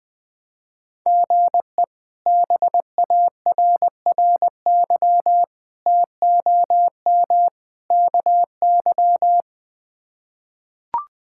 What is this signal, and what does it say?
Morse code, heard as a keyed tone, about 20 wpm, 700 Hz. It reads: GE BARRY TOM KY
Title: K1USN SST QSO Practice - 20wpm